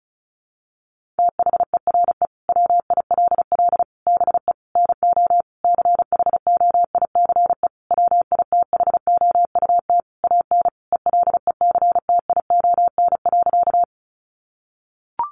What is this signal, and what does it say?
Morse code, heard as a keyed tone, about 34 wpm, 700 Hz. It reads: THERE WILL BE NO CHOICE WITHOUT AN ELECTION.
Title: Sentences from Top 1500 Words 35wpm